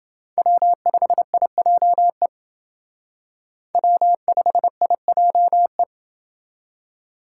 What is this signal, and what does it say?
Morse code, heard as a keyed tone, about 28 wpm, 700 Hz. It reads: W5IJE W5IJE